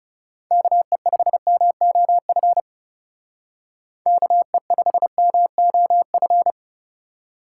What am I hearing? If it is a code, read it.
Morse code, heard as a keyed tone, about 35 wpm, 700 Hz. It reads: KE5MOF KE5MOF
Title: Speed Racing - USA Call Signs 30wpm